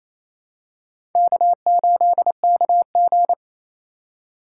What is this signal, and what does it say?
Morse code, heard as a keyed tone, about 29 wpm, 700 Hz. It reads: K8KG
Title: USA Call Signs - 4 Characters - 2x Character Spacing - 28wpm